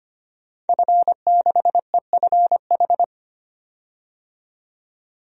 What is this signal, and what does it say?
Morse code, heard as a keyed tone, about 25 wpm, 690 Hz. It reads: F6EFH
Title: DXCC-100 Call Signs - 5 Characters - 25wpm